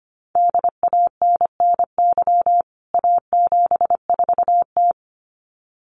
Morse code, heard as a keyed tone, about 25 wpm, 700 Hz. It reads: DANNY A74T
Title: CWT Exchange 25wpm